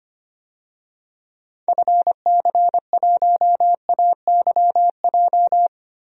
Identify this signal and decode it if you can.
Morse code, heard as a keyed tone, about 25 wpm, 700 Hz. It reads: FC1AYJ
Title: DXCC-100 Call Signs - 6 Characters - Repeated 3x - 25wpm